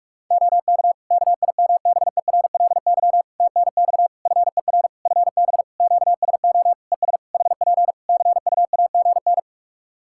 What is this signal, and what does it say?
Morse code, heard as a keyed tone, about 46 wpm, 700 Hz. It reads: OK KIMBERLY TNX FER FB QSO ES HP CUAGN
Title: Practice QSO - Exchange 4 - Ending - 45wpm